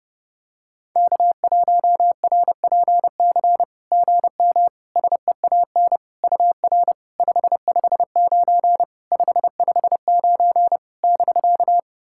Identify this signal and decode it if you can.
Morse code, heard as a keyed tone, about 30 wpm, 700 Hz. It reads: K1RPC GM SEAN UR 559 559 <BK>
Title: SOTA QSO Practice 30wpm